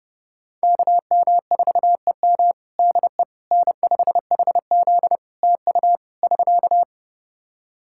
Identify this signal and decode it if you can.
Morse code, heard as a keyed tone, about 32 wpm, 700 Hz. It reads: KM4EM DE N5HZ TU <SK>
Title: Practice QSO - Exchange 4 - Ending - 30wpm